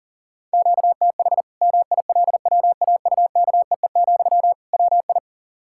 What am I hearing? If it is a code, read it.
Morse code, heard as a keyed tone, about 40 wpm, 700 Hz. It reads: QTH MILWAUKEE, WI
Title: Practice QSO - Exchange 1 - Beginning - 40wpm